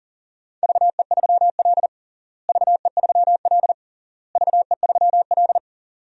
Morse code, heard as a keyed tone, about 40 wpm, 700 Hz. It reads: VE3L VE3L VE3L
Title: DXCC-100 Call Signs - 4 Characters - Repeated 3x - 40wpm